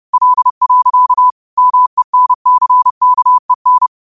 Morse code, heard as a keyed tone, about 30 wpm, 1000 Hz. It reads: L. MENCKEN
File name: qotdfiles / 30wpm